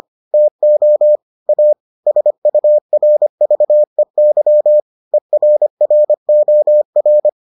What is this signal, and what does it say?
Morse code, heard as a keyed tone, about 25 wpm, 600 Hz. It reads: TO A SURVEY ERROR